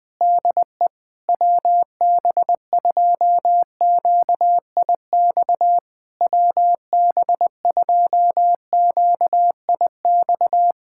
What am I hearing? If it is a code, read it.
Morse code, heard as a keyed tone, about 20 wpm, 700 Hz. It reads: DE WB2QIX WB2QIX
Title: Speed-Racing Common QSO Phrases - 17wpm